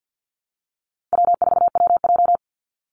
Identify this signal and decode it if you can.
Morse code, heard as a keyed tone, about 50 wpm, 700 Hz. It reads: W4WJ
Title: Easy Call Signs Repeated 2x - 50wpm